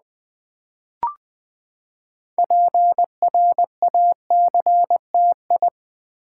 Morse code, heard as a keyed tone, about 20 wpm, 700 Hz. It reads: PRACTI
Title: Build A Word - 6 Letters At A Time - 12-Letter Words - 20wpm